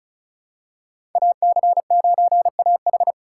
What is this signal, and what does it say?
Morse code, heard as a keyed tone, about 34 wpm, 700 Hz. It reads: AC9AH
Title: USA Call Signs - 4x Character Spacing - 35wpm